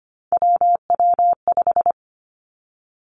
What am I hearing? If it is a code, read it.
Morse code, heard as a keyed tone, about 25 wpm, 700 Hz. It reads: WW5
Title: Three Challenging Letter-Number Combinations 25wpm